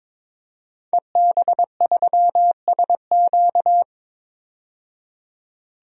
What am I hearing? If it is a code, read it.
Morse code, heard as a keyed tone, about 22 wpm, 700 Hz. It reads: EB3SQ
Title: DXCC-100 Call Signs - 5 Characters - 22wpm